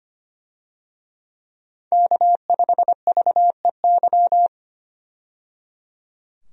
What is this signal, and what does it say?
Morse code, heard as a keyed tone, about 25 wpm, 700 Hz. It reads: K5VEY
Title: USA Call Signs - 5 Characters - 25wpm